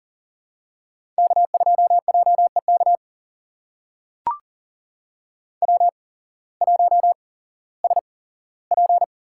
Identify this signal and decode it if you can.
Morse code, heard as a keyed tone, about 38 wpm, spaced as for 18 wpm, 700 Hz. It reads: K2JEK W1SP
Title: USA Call Signs - 5 Characters - 8x Character Spacing - 40wpm